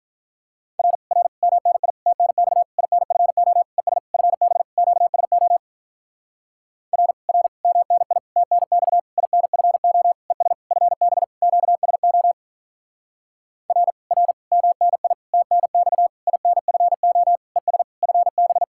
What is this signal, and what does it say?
Morse code, heard as a keyed tone, about 48 wpm, 700 Hz. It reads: R R MNI TNX INFO ES FB QSO R R MNI TNX INFO ES FB QSO R R MNI TNX INFO ES FB